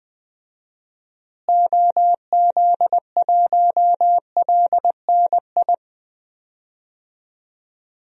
Morse code, heard as a keyed tone, about 21 wpm, 700 Hz. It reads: OZ1LNI
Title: DXCC-100 Call Signs - 6 Characters - Repeated 3x - 20wpm